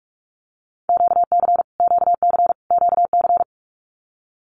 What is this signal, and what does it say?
Morse code, heard as a keyed tone, about 44 wpm, 700 Hz. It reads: QC QC QC
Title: Canadian Provinces and Territories - Repeated 3x - 45wpm